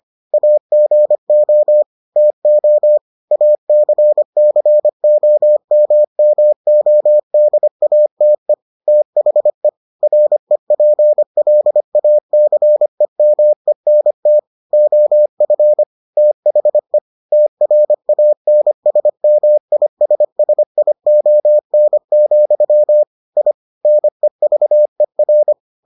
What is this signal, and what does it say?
Morse code, heard as a keyed tone, about 25 wpm, 600 Hz. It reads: AGO TO ACCOMMODATE THE REPLACEMENT OF THE TRANSMISSION, I NEVER